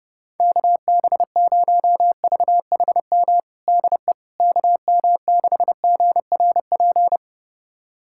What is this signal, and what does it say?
Morse code, heard as a keyed tone, about 30 wpm, 700 Hz. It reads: KB0VHM DE KM6GRP
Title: USA Callsign DE Callsign - 6 Characters - 30wpm